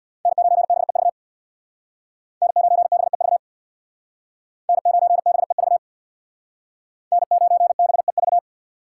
Morse code, heard as a keyed tone, about 62 wpm, 700 Hz. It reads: N9BEV N9BEV N9BEV N9BEV
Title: Speed Racing - USA 1x3 Call Signs 50wpm